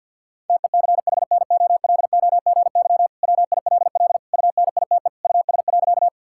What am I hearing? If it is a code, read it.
Morse code, heard as a keyed tone, about 51 wpm, 700 Hz. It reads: TECHNOLOGY WILL UNITE US.
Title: Sentences from Top 1000 Words - Review Part 5 - 50wpm